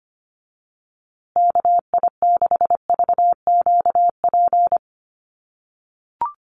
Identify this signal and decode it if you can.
Morse code, heard as a keyed tone, about 25 wpm, 700 Hz. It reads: KI6VQP
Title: USA Call Signs Repeated 3x 25wpm